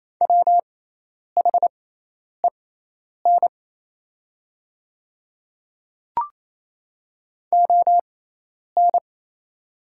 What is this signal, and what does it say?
Morse code, heard as a keyed tone, about 28 wpm, spaced as for 10 wpm, 700 Hz. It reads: WHEN ON
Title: 6x Character Spacing - Top 100 Words - 28wpm